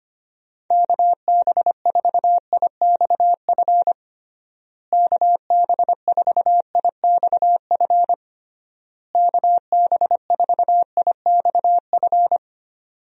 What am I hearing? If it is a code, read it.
Morse code, heard as a keyed tone, about 25 wpm, 700 Hz. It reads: KB4IXF KB4IXF KB4IXF
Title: USA Call Signs - 6 Characters - Repeated 3x - 25wpm